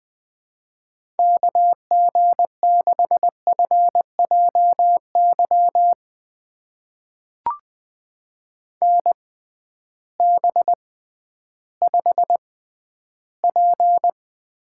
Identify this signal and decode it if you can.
Morse code, heard as a keyed tone, about 20 wpm, spaced as for 12 wpm, 700 Hz. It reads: KG6FJY NB5P
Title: USA Call Signs - 6x Character Spacing - 20wpm